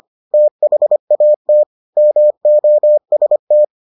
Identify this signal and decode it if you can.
Morse code, heard as a keyed tone, about 25 wpm, 600 Hz. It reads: THAT MOST